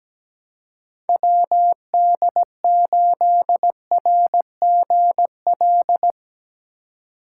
Morse code, heard as a keyed tone, about 18 wpm, 700 Hz. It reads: WD8RGL
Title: Speed Racing - USA 2x3 Call Signs 17wpm